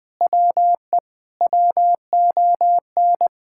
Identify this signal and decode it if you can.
Morse code, heard as a keyed tone, about 20 wpm, 700 Hz. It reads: WE WON
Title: Sentences with 8-Word Phrases - 20wpm